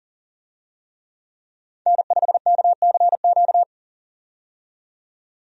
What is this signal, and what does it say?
Morse code, heard as a keyed tone, about 40 wpm, 700 Hz. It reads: N5KCQ
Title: USA Call Signs - 5 Characters - 40wpm